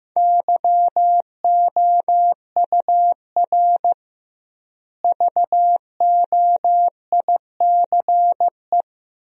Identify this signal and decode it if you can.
Morse code, heard as a keyed tone, about 15 wpm, 700 Hz. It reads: YOUR VOICE
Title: 2x Word-Spacing - Sentences from Top 500 Words - 15wpm